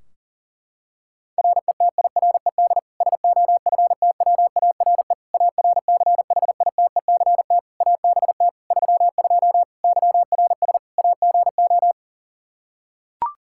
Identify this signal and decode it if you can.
Morse code, heard as a keyed tone, about 40 wpm, 700 Hz. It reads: RETIRED SOFTWARE ARCHITECT ABT 32 YRS AGO
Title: Practice QSO - Exchange 3 - More Info - 40wpm